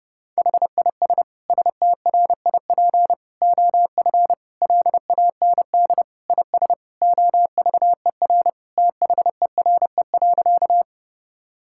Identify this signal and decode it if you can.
Morse code, heard as a keyed tone, about 30 wpm, 700 Hz. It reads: HIS STRIP OF LAND IS OVER THERE.
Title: Sentences from Top 1600 Words 30wpm